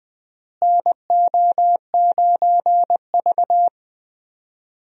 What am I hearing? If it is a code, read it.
Morse code, heard as a keyed tone, about 20 wpm, 700 Hz. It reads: NO9V
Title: USA Call Signs - 4 Characters - Repeated 2x - 20wpm